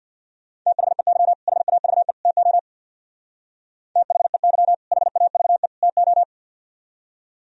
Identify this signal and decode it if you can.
Morse code, heard as a keyed tone, about 61 wpm, 700 Hz. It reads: THEY HAVE TO THEY HAVE TO
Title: Speed-Racing - Sets of 3 Words 50wpm